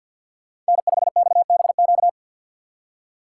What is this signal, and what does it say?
Morse code, heard as a keyed tone, about 60 wpm, 700 Hz. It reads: N5XBQ
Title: Speed Racing - USA Call Signs 50wpm